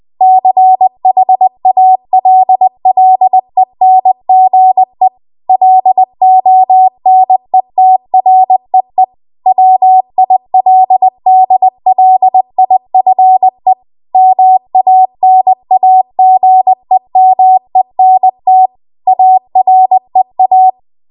Morse code, heard as a keyed tone, about 20 wpm, 750 Hz. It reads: CHALLENGE LONETREE WILDLIFE MANAGEMENT AREA
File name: Archive / 20 WPM / 20WPM